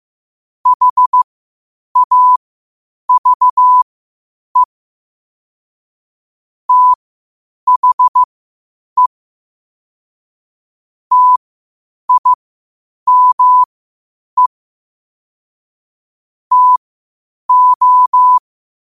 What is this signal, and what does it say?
Morse code, heard as a keyed tone, about 15 wpm, spaced as for 8 wpm, 1000 Hz. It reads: HAVE THE TIME TO
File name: qotdfiles / 8wpm